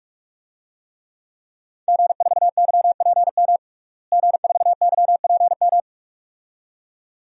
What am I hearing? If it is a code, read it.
Morse code, heard as a keyed tone, about 41 wpm, 700 Hz. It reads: G4YPM G4YPM
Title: DXCC-100 Call Signs - 5 Characters - Repeated 2x - 45wpm